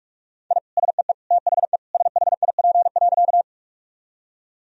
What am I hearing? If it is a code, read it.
Morse code, heard as a keyed tone, about 43 wpm, 700 Hz. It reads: I SEE THE SHIP.
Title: Sentences from Top 1000 Words - Review Part 2 - 45wpm